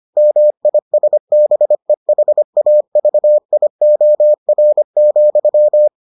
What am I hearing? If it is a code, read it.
Morse code, heard as a keyed tone, about 25 wpm, 600 Hz. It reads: MISBEHAVIOR,